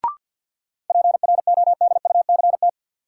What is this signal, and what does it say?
Morse code, heard as a keyed tone, about 53 wpm, 700 Hz. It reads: PRODUCT
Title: Sentences from Top 1400 Words - New Word Drill Alt - 50wpm